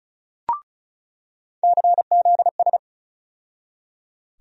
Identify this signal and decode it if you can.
Morse code, heard as a keyed tone, about 36 wpm, 700 Hz. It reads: CZS
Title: USA Call Sign Postfixes - 3 Letters 35wpm